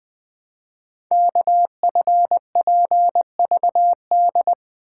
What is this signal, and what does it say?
Morse code, heard as a keyed tone, about 20 wpm, 700 Hz. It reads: KFPVD
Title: Code Groups - 5 Letters - 20wpm